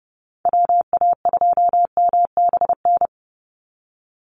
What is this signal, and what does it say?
Morse code, heard as a keyed tone, about 30 wpm, 700 Hz. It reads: WA2MBN
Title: USA Call Signs 30wpm